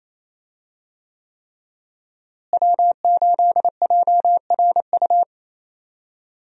Morse code, heard as a keyed tone, about 28 wpm, 700 Hz. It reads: W8JRU
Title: USA Call Signs - 5 Characters - Repeated 3x - 28wpm